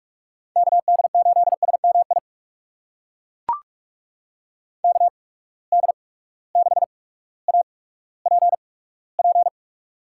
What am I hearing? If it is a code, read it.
Morse code, heard as a keyed tone, about 42 wpm, spaced as for 19 wpm, 700 Hz. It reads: KD8SMI KD6APP
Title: USA Call Signs - 6 Characters - 8x Character Spacing - 45wpm